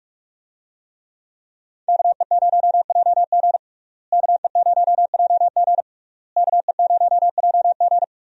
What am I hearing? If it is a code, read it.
Morse code, heard as a keyed tone, about 43 wpm, 700 Hz. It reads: KE0JG KE0JG KE0JG
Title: USA Call Signs - 5 Characters - Repeated 3x - 45wpm